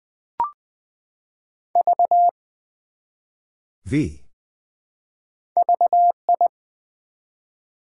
Morse code, heard as a keyed tone, about 20 wpm, 700 Hz. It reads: V VI
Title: Build A Word - 1 Letter At A Time - 3-Letter Words - 20wpm